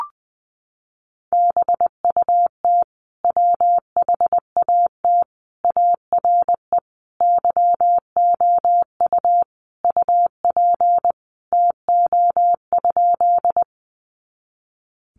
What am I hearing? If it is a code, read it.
Morse code, heard as a keyed tone, about 20 wpm, 700 Hz. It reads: BUT WHAT ARE YOU UP TO?
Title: Sentences from Top 100 Words 20wpm